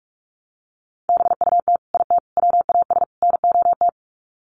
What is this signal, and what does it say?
Morse code, heard as a keyed tone, about 44 wpm, 700 Hz. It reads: BUT IT WAS NOT
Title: Sets of 4 Words 45wpm